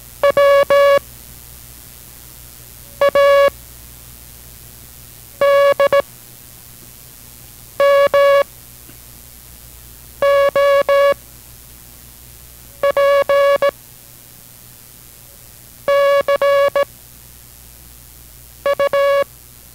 Morse code, heard as a keyed tone, about 17 wpm, 560 Hz. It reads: W A D M O P C U